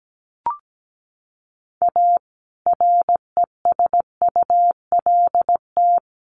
Morse code, heard as a keyed tone, about 17 wpm, 700 Hz. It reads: A RESULT
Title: Sets of 3 Words 17wpm